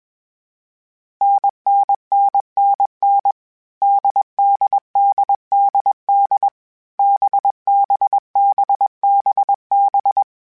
Morse code, heard as a keyed tone, about 21 wpm, 800 Hz. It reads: NNNNN DDDDD BBBBB
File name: Difficult Combos / Difficult Combos-mp3s / NDB NDB6 @20